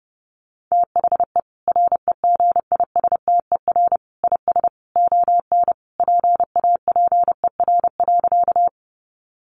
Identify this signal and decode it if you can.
Morse code, heard as a keyed tone, about 30 wpm, 700 Hz. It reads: THE REGISTER IS ON PAPER.
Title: Sentences from Top 1200 Words 30wpm